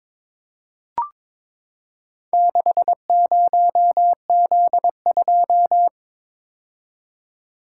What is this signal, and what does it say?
Morse code, heard as a keyed tone, about 23 wpm, 700 Hz. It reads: 60Z2
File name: practice / 22wpm / Four Letter-Number Combinations - 22wpm